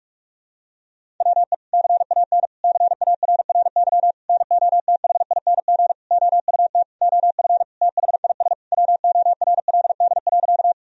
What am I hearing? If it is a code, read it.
Morse code, heard as a keyed tone, about 41 wpm, 700 Hz. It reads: WE CAN CARRY NOTHING OUT OF THIS WORLD.
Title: Sentences from Top 1000 Words - Review Part 1 - 45wpm